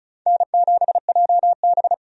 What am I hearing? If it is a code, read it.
Morse code, heard as a keyed tone, about 34 wpm, 700 Hz. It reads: N7JB
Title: USA Call Signs - 4 Characters - 35wpm